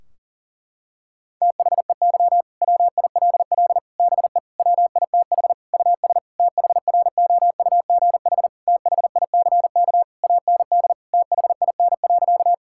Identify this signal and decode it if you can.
Morse code, heard as a keyed tone, about 40 wpm, 700 Hz. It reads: THEY WILL BE WITH US THROUGH THICK AND THIN.
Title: Sentences from Top 1600 Words 40wpm